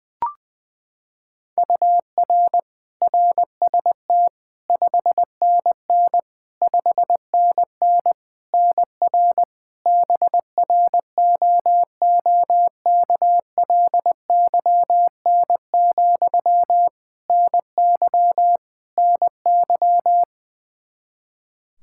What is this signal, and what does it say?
Morse code, heard as a keyed tone, about 20 wpm, 700 Hz. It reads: UR RST 5NN 5NN NR BROOKLYN, NY NY
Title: K1USN SST QSO Practice 20wpm